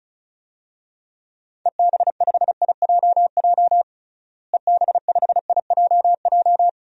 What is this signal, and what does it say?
Morse code, heard as a keyed tone, about 36 wpm, 700 Hz. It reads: EB5IJJ EB5IJJ
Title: DXCC-100 Call Signs - 6 Characters - Repeated 2x - 35wpm